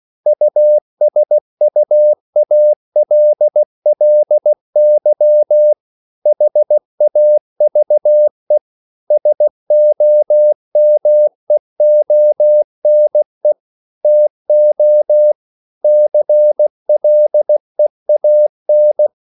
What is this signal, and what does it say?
Morse code, heard as a keyed tone, about 16 wpm, 600 Hz. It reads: USUALLY HAVE SOMEONE TO CLEAN